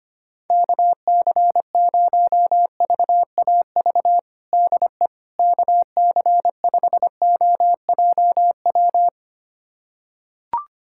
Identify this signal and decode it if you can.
Morse code, heard as a keyed tone, about 25 wpm, 700 Hz. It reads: KC0VAV DE KC5OJW
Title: USA Callsign DE Callsign 25wpm